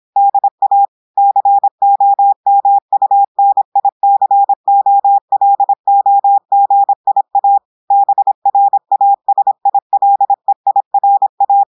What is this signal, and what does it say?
Morse code, heard as a keyed tone, about 26 wpm, 800 Hz. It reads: DA COMUNICOLOGIA BRASILEIRA
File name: audios / portugais / portugais texte en clair 1 vitesse 26 mots minutes